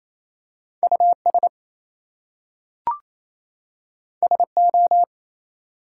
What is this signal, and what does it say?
Morse code, heard as a keyed tone, about 28 wpm, 700 Hz. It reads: US SO
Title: Two Letter Words - 28wpm